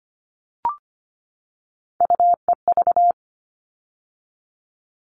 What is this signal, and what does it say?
Morse code, heard as a keyed tone, about 25 wpm, 700 Hz. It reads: UEV
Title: All Cut Numbers - 3-Digit 25wpm